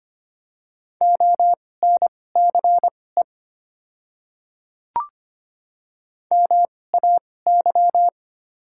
Morse code, heard as a keyed tone, about 25 wpm, spaced as for 18 wpm, 700 Hz. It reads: ONCE MAY
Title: 2x Character Spacing - Top 300 Words - 25wpm